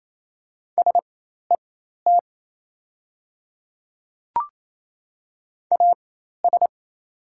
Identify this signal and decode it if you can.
Morse code, heard as a keyed tone, about 28 wpm, spaced as for 13 wpm, 700 Hz. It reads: SET AS
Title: 4x Character Spacing - Top 300 Words - 28wpm